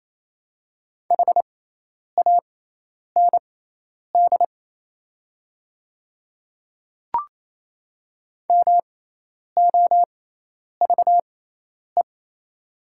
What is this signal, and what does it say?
Morse code, heard as a keyed tone, about 27 wpm, spaced as for 10 wpm, 700 Hz. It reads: HAND MOVE
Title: 6x Character Spacing - Top 200 Words - 28wpm